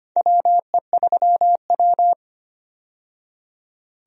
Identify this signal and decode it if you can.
Morse code, heard as a keyed tone, about 25 wpm, 700 Hz. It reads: WE3W
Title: USA Call Signs - 4 Characters - 25wpm